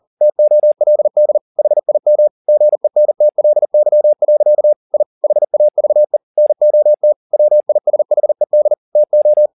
Morse code, heard as a keyed tone, about 40 wpm, 600 Hz. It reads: TOLD HIM GENTLY. I HAVE NOT WISHED TO